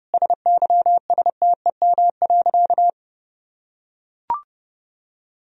Morse code, heard as a keyed tone, about 30 wpm, 700 Hz. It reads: SYSTEM.
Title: Sentences from Top 1000 Words - Review Part 3 - 30wpm